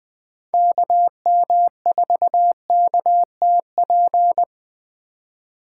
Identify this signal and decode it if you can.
Morse code, heard as a keyed tone, about 21 wpm, 700 Hz. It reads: KM4KTP
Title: Speed Racing - USA 2x3 Call Signs 17wpm